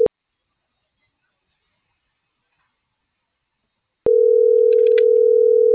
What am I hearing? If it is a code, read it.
Ringback tone (North America), 440 + 480 Hz.